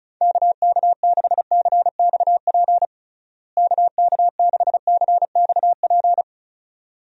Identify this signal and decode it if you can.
Morse code, heard as a keyed tone, about 36 wpm, 700 Hz. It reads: KK6CXP KK6CXP
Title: USA Call Signs - 6 Characters - Repeated 2x - 35wpm